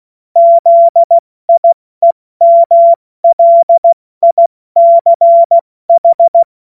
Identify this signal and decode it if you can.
Morse code, heard as a keyed tone, about 16 wpm, 680 Hz. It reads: ZIEMLICH
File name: Morsekurs / Witze / Witz 05 Dezember 2023 70 F80